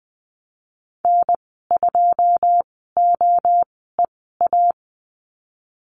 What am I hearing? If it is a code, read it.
Morse code, heard as a keyed tone, about 20 wpm, spaced as for 15 wpm, 700 Hz. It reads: N2OEA
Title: 082 Lesson 14g - Number 2 - Call Signs 20wpm